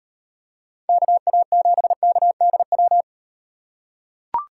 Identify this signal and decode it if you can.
Morse code, heard as a keyed tone, about 38 wpm, 700 Hz. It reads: KA7KDW